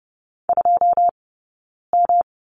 Morse code, heard as a keyed tone, about 30 wpm, 700 Hz. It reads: 2 M
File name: Lesson 20d - Number 3 - Two Characters 30wpm